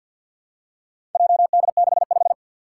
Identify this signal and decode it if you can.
Morse code, heard as a keyed tone, about 52 wpm, 700 Hz. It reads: JD65
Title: Four Challenging Letter-Number Combinations 50wpm